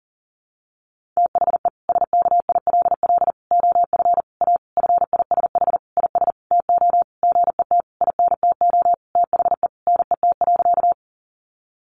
Morse code, heard as a keyed tone, about 40 wpm, 700 Hz. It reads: THE SKILL OF A FISH IS TO GET INTO THE NET.
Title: Sentences from Top 1400 Words 40wpm